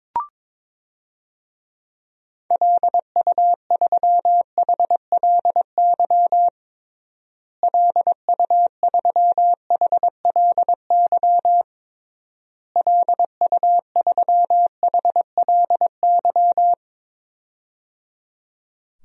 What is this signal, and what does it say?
Morse code, heard as a keyed tone, about 22 wpm, 700 Hz. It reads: LU3HLY LU3HLY LU3HLY
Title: DXCC-100 Call Signs - 6 Characters - Repeated 3x - 22wpm